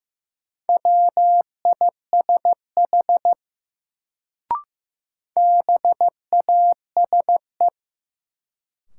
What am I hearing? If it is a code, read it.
Morse code, heard as a keyed tone, about 15 wpm, 700 Hz. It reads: WISH BASE
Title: Sentences from Top 1600 Words - New Word Drill Alt - 15wpm